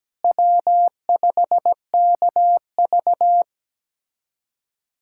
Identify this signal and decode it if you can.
Morse code, heard as a keyed tone, about 17 wpm, 700 Hz. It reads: W5KV
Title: USA Callsign DE Callsign - 4 Characters - 17wpm